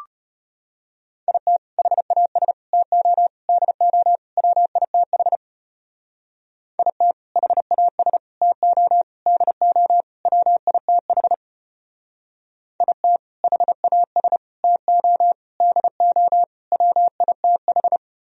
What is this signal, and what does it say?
Morse code, heard as a keyed tone, about 33 wpm, 700 Hz. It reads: IT HAS TO DO WITH IT HAS TO DO WITH IT HAS TO DO WITH